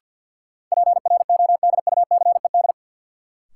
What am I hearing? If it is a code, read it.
Morse code, heard as a keyed tone, about 53 wpm, 700 Hz. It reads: PRODUCED
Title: Sentences from Top 1000 Words - Review Part 6 - New Word Drill - 50wpm